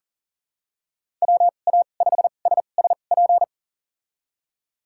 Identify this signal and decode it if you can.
Morse code, heard as a keyed tone, about 40 wpm, spaced as for 29 wpm, 700 Hz. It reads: WA5SSP
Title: USA Call Signs - 6 Characters - 2x Character Spacing - 40wpm